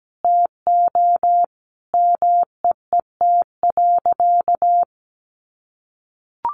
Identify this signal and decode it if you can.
Morse code, heard as a keyed tone, about 17 wpm, 700 Hz. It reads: TO MEET.
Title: Sentences from Top 1200 Words 17wpm